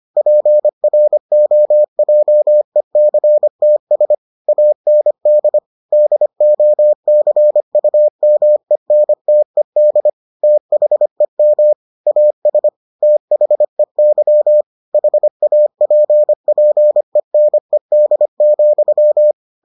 Morse code, heard as a keyed tone, about 25 wpm, 600 Hz. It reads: PROJECTS AND DOCUMENTED THEM AS THEY HAPPENED,